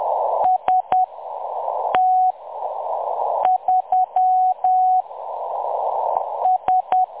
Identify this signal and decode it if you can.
Morse code, heard as a keyed tone, about 10 wpm, spaced as for 6 wpm, 740 Hz. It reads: ST3 S
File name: M23-ST3- -0858utc-10184khz